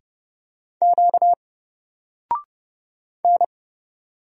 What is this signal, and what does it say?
Morse code, heard as a keyed tone, about 30 wpm, 700 Hz. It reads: Q N
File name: Single-Letter-Number-30wpm